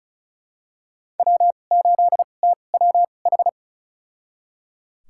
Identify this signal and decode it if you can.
Morse code, heard as a keyed tone, about 36 wpm, spaced as for 25 wpm, 700 Hz. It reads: W8TWH
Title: USA Call Signs - 2x Character Spacing - 35wpm